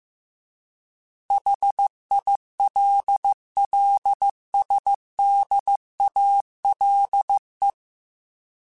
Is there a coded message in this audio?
Yes: Morse code from 1.3 to 7.7 s, heard as a keyed tone, about 15 wpm, 800 Hz. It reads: HILLSDALE